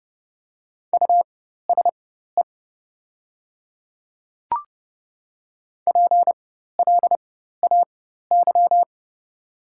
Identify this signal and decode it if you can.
Morse code, heard as a keyed tone, about 30 wpm, spaced as for 14 wpm, 700 Hz. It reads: USE PLAY
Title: 4x Character Spacing - Top 300 Words - 30wpm